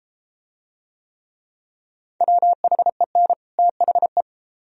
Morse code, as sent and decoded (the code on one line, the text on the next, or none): .-- .... -. / - .... .
WHEN THE